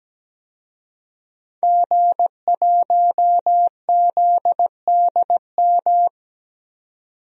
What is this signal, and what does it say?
Morse code, heard as a keyed tone, about 17 wpm, 700 Hz. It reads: G1ZDM
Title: DXCC-100 Call Signs - 5 Characters - 17wpm